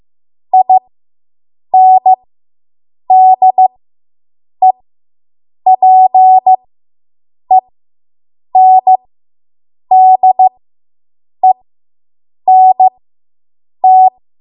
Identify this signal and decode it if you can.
Morse code, heard as a keyed tone, about 15 wpm, spaced as for 7 wpm, 750 Hz. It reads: INDEPENDENT